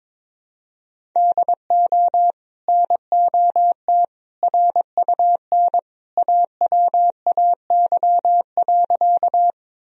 Morse code, heard as a keyed tone, about 22 wpm, 700 Hz. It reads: DO NOT RUN AWAY.
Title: Sentences from Top 1000 Words - Review Part 1 - 22wpm